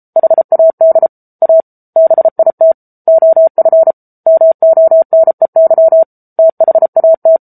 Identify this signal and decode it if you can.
Morse code, heard as a keyed tone, about 34 wpm, 650 Hz. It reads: HAD A BIT OF MONEY THAT
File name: Book Two / 07-Chapter 7 - 33 WPM